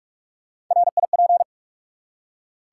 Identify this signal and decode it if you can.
Morse code, heard as a keyed tone, about 44 wpm, 700 Hz. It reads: RIP
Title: Three Letter Words - 45wpm